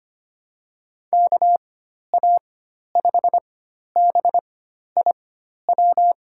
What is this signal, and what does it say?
Morse code, heard as a keyed tone, about 25 wpm, spaced as for 12 wpm, 700 Hz. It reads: KA5BIW
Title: USA Call Signs - 6 Characters - 4x Character Spacing - 25wpm